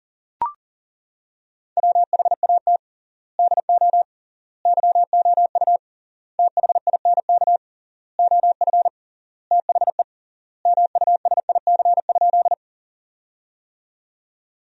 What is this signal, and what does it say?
Morse code, heard as a keyed tone, about 40 wpm, 700 Hz. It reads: WHAT DO YOU THINK OF THE MUSIC?
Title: 3x Word-Spacing - Sentences from Top 300 Words - 40wpm